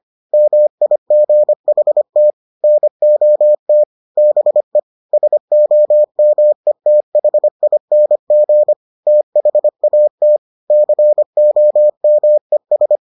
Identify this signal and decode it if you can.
Morse code, heard as a keyed tone, about 25 wpm, 600 Hz. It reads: MIGHT NOT BE SOMETHING THAT COMES